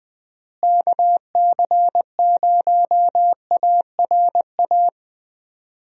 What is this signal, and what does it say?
Morse code, heard as a keyed tone, about 20 wpm, 700 Hz. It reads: KC0ARA
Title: USA Call Signs - 6 Characters - Repeated 2x - 20wpm